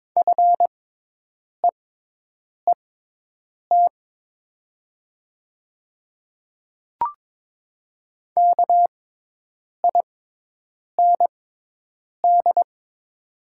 Morse code, heard as a keyed tone, about 22 wpm, spaced as for 8 wpm, 700 Hz. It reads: FEET KIND
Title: 6x Character Spacing - Top 300 Words - 22wpm